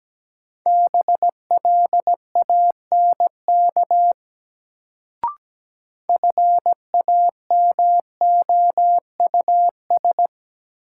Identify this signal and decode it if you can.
Morse code, heard as a keyed tone, about 17 wpm, 700 Hz. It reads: BLANK FAMOUS
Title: Sentences from Top 1500 Words - New Word Drill - 17wpm